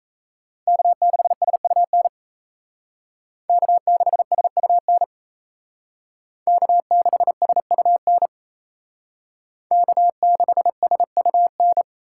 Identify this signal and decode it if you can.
Morse code, heard as a keyed tone, about 37 wpm, 700 Hz. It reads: K6SUN K6SUN K6SUN K6SUN